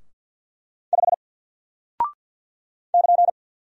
Morse code, heard as a keyed tone, about 48 wpm, 700 Hz. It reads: 5 <KN>
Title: Single Character (Letter:Number:Punctuation:Prosign) Alt 50wpm